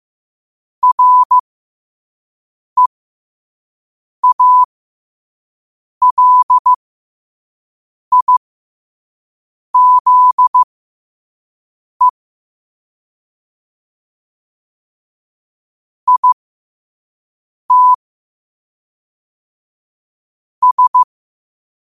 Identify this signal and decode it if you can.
Morse code, heard as a keyed tone, about 15 wpm, spaced as for 5 wpm, 1000 Hz. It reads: REALIZE IT S
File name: qotdfiles / 5wpm